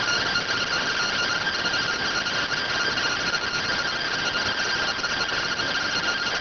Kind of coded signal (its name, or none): two-tone teleprinter signal